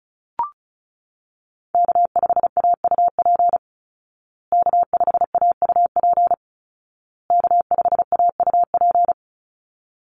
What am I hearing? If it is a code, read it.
Morse code, heard as a keyed tone, about 34 wpm, 700 Hz. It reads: K5AUP K5AUP K5AUP
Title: Call Signs Repeated 2x - 35wpm